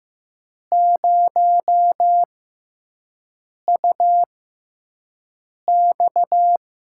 Morse code, heard as a keyed tone, about 15 wpm, spaced as for 5 wpm, 700 Hz. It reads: 0UX
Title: USA Call Signs - 5 Characters - 6x Character Spacing - 15wpm